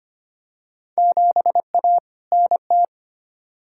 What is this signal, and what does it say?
Morse code, heard as a keyed tone, about 25 wpm, 700 Hz. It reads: 7A NT
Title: Field Day Exchanges Encore - 25wpm